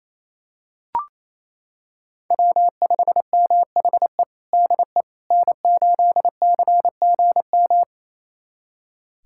Morse code, heard as a keyed tone, about 28 wpm, 700 Hz. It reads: W5MHE DE N8CGM
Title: USA Callsign DE Callsign - 5 Characters - 28wpm